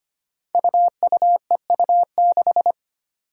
Morse code, heard as a keyed tone, about 25 wpm, 700 Hz. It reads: UUEU6
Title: Five Challenging Letter-Number Combinations 25wpm